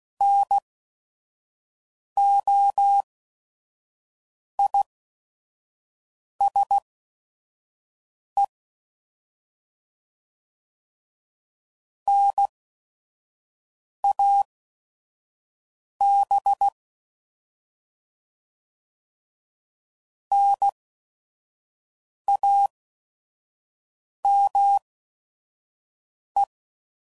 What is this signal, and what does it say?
Morse code, heard as a keyed tone, about 16 wpm, spaced as for 5 wpm, 800 Hz. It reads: NOISE NAB NAME
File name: Prefixes / By First Letter / Ns